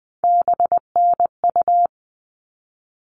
Morse code, heard as a keyed tone, about 20 wpm, 700 Hz. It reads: BNU
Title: All Cut Numbers - 3-Digit 20wpm